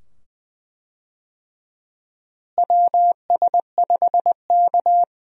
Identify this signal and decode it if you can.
Morse code, heard as a keyed tone, about 20 wpm, 700 Hz. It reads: WS5K
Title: USA Call Signs - 4 Characters - Repeated 2x - 20wpm